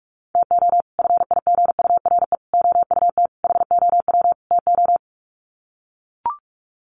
Morse code, heard as a keyed tone, about 44 wpm, 700 Hz. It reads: TO FIGURE OUT HOW TO
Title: Sets of 5 Words 45wpm